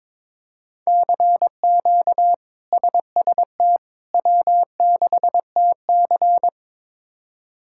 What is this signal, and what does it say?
Morse code, heard as a keyed tone, about 22 wpm, 700 Hz. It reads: CQ SST W6TC
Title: K1USN SST QSO Practice - 22wpm